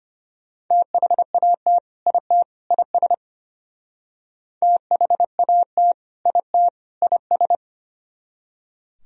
Morse code, heard as a keyed tone, about 32 wpm, 700 Hz. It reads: THAT IT IS THAT IT IS